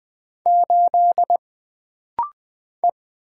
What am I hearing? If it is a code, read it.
Morse code, heard as a keyed tone, about 20 wpm, 700 Hz. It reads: 8 E
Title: Single Character (Letter:Number:Punctuation:Prosign) - Rapid-Fire - 20wpm